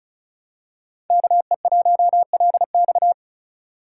Morse code, heard as a keyed tone, about 36 wpm, 700 Hz. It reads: KE1LX
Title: USA Call Signs - 5 Characters - 2x Character Spacing - 35wpm